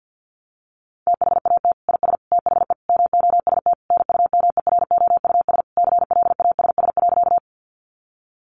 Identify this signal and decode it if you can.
Morse code, heard as a keyed tone, about 50 wpm, 700 Hz. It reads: THAT IS THE MOST NUMEROUS CLASS.
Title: Sentences from Top 1400 Words 50wpm